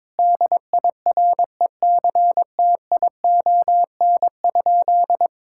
Morse code, heard as a keyed tone, about 22 wpm, 700 Hz. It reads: DIRECTION?
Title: Sentences from Top 1000 Words - Review Part 5 - 22wpm